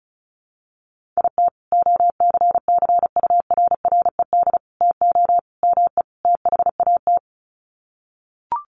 Morse code, heard as a keyed tone, about 34 wpm, 700 Hz. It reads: IT OCCURRED TO ME THAT
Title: Sets of 5 Words Encore 35wpm